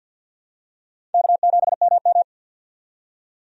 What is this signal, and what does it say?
Morse code, heard as a keyed tone, about 52 wpm, 700 Hz. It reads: K7MM
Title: USA Call Signs - 4 Characters - 50wpm